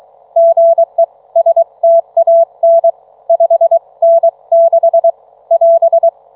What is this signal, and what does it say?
Morse code, heard as a keyed tone, about 25 wpm, spaced as for 22 wpm, 660 Hz. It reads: GE STAN 5N6 <AS>